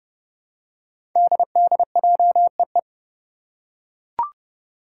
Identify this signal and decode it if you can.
Morse code, heard as a keyed tone, about 30 wpm, 700 Hz. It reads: DDJEE
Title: Five Challenging Letter-Number Combinations 30wpm